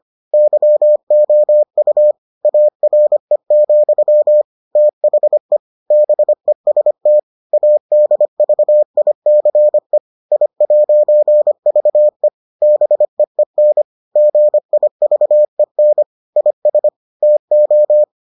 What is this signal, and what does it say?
Morse code, heard as a keyed tone, about 25 wpm, 600 Hz. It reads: YOU ARE, THE BEST ADVICE I'VE BEEN GIVEN IS TO